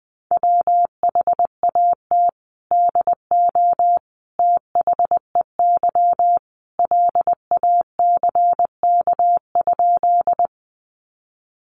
Morse code, heard as a keyed tone, about 19 wpm, 700 Hz. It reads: WHAT DO THEY LACK?
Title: Sentences from Top 1100 Words 20wpm